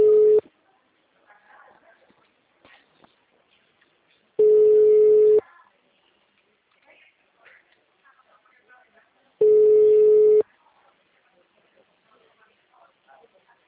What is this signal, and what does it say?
Ringback tone (Europe, 425 Hz).